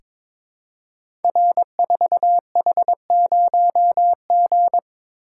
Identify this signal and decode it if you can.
Morse code, heard as a keyed tone, about 22 wpm, 700 Hz. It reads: R4H0G
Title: Code Groups - 5 Letter:Numbers - 22wpm